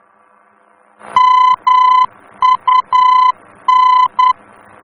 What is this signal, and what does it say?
Morse code, heard as a keyed tone, about 10 wpm, 1010 Hz. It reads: MUN
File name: germany / mx / MUN-D-VOR